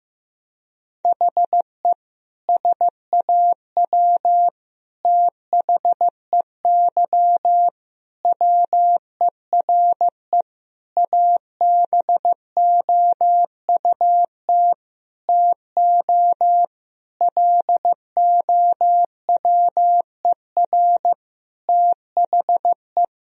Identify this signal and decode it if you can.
Morse code, heard as a keyed tone, about 15 wpm, 700 Hz. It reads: HE SAW THEY WERE ABOUT TO LOWER THE
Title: Sentences from Top 1000 Words - Review Part 4 - 15wpm